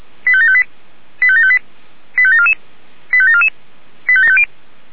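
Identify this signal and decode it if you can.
Five-tone selective call (ZVEI), heard as quick runs of five tones, tones of about 70 ms. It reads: 85758 85758 85744 85744 85750
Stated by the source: ZVEI five-tone selective call 85735 received by SDR.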